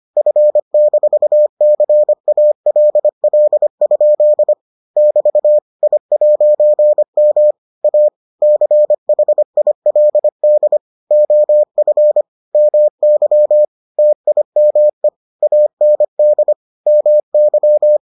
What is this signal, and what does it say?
Morse code, heard as a keyed tone, about 25 wpm, 600 Hz. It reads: F-CALL? <BT> I'M A CHILD OF MY TIME AND MY